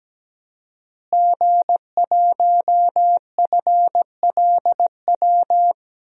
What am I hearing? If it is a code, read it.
Morse code, heard as a keyed tone, about 17 wpm, 700 Hz. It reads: G1FLW
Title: DXCC-100 Call Signs - 5 Characters - Repeated 3x - 17wpm